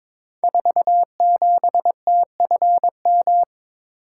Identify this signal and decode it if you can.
Morse code, heard as a keyed tone, about 22 wpm, 700 Hz. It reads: 47TFM
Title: Code Groups - 5 Letter:Numbers - 22wpm